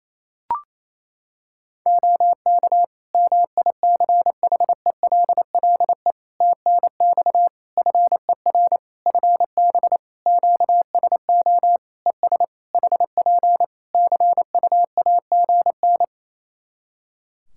Morse code, heard as a keyed tone, about 28 wpm, 700 Hz. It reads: OK MICHELLE TNX FER FB QSO ES HP CUAGN
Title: Practice QSO - Exchange 4 - Ending - 28wpm